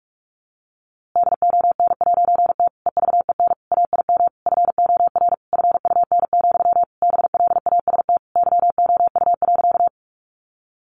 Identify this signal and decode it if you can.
Morse code, heard as a keyed tone, about 44 wpm, 700 Hz. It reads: DON'T EVEN AIM FOR FUN, BLAST YOU.
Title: A Sound Of Thunder 45wpm